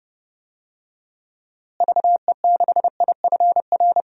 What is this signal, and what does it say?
Morse code, heard as a keyed tone, about 29 wpm, 700 Hz. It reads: VE6IFR
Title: DXCC-100 Call Signs - 6 Characters - Repeated 3x - 30wpm